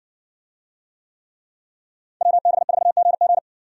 Morse code, heard as a keyed tone, about 60 wpm, 700 Hz. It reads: WB4GG